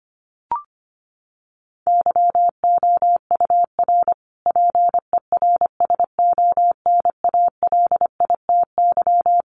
Morse code, heard as a keyed tone, about 25 wpm, 700 Hz. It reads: YOUR PERSONALITY
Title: Sentences from Top 1400 Words 25wpm